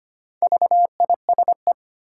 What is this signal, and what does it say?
Morse code, heard as a keyed tone, about 25 wpm, 690 Hz. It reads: VISE
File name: practice / 25wpm / Difficult Words with Hints - 25wpm